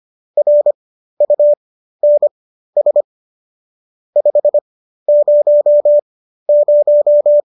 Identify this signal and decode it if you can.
Morse code, heard as a keyed tone, about 25 wpm, spaced as for 13 wpm, 590 Hz. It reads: RUNS 500